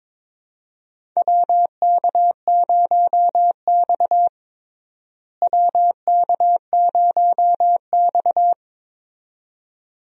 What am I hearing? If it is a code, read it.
Morse code, heard as a keyed tone, about 21 wpm, 700 Hz. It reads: WK0X WK0X
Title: USA Call Signs - 4 Characters - Repeated 2x - 22wpm